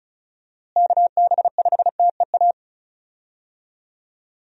Morse code, heard as a keyed tone, about 36 wpm, 700 Hz. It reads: KB5TEA
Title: USA Call Signs - 6 Characters - Repeated 3x - 35wpm